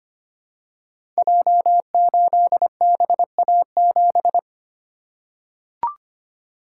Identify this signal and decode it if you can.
Morse code, heard as a keyed tone, about 25 wpm, 700 Hz. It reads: J8BA7
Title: Code Groups - 5 Letter:Numbers - 25wpm